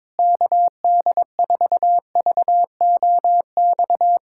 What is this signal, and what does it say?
Morse code, heard as a keyed tone, about 22 wpm, 700 Hz. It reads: KD4VOX
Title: USA Callsign DE Callsign - 6 Characters - 22wpm